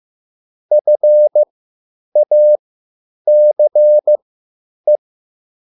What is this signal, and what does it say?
Morse code, heard as a keyed tone, about 15 wpm, spaced as for 9 wpm, 600 Hz. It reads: FACE
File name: QSL / CW lesson